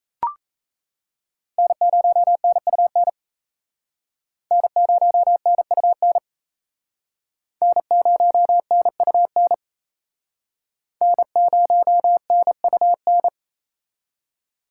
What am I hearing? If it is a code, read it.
Morse code, heard as a keyed tone, about 35 wpm, 700 Hz. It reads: N0NUN N0NUN N0NUN N0NUN